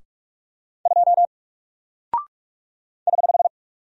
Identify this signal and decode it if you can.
Morse code, heard as a keyed tone, about 44 wpm, 700 Hz. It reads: 2 <HH>
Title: Single Character (Letter:Number:Punctuation:Prosign) 45wpm